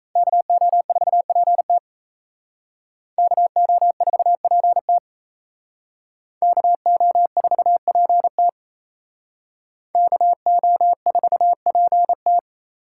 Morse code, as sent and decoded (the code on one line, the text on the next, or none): -.- --- ....- .--. - / -.- --- ....- .--. - / -.- --- ....- .--. - / -.- --- ....- .--. -
KO4PT KO4PT KO4PT KO4PT